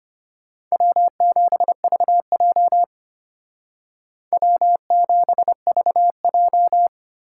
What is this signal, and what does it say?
Morse code, heard as a keyed tone, about 31 wpm, 700 Hz. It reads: W7VJ W7VJ
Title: Speed Racing - USA Call Signs 25wpm